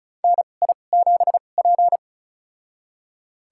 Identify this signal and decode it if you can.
Morse code, heard as a keyed tone, about 34 wpm, spaced as for 25 wpm, 700 Hz. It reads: NI7P
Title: USA Call Signs - 2x Character Spacing - 35wpm